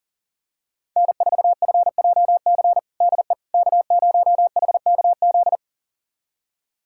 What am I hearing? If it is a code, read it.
Morse code, heard as a keyed tone, about 40 wpm, 700 Hz. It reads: N4FJC DE K0HKZ
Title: USA Callsign DE Callsign - 5 Characters - 40wpm